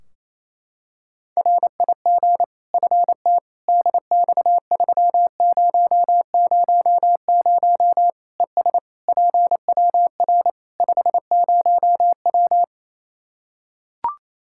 Morse code, heard as a keyed tone, about 28 wpm, 700 Hz. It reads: RIG FT DX3000 ES PWR 50W
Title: Practice QSO - Exchange 2 - Info - 28wpm